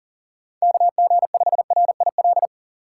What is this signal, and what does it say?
Morse code, heard as a keyed tone, about 40 wpm, 700 Hz. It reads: KG5RIL
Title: USA Call Signs - 6 Characters - 2x Character Spacing - 40wpm